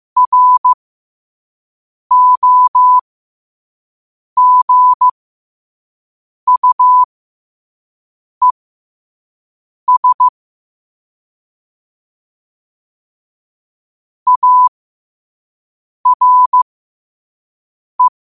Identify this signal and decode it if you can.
Morse code, heard as a keyed tone, about 15 wpm, spaced as for 5 wpm, 1000 Hz. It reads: ROGUES ARE
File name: qotdfiles / 5wpm